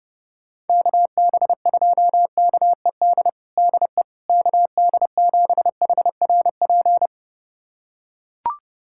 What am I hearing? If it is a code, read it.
Morse code, heard as a keyed tone, about 32 wpm, 700 Hz. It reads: KB2KED DE KD7HRP
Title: USA Callsign DE Callsign 30wpm